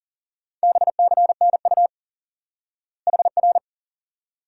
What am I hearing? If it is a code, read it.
Morse code, heard as a keyed tone, about 37 wpm, 700 Hz. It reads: BCNU HR
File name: IQE Course 3 - Lesson 89 - YAGI - Two Elements - 40wpm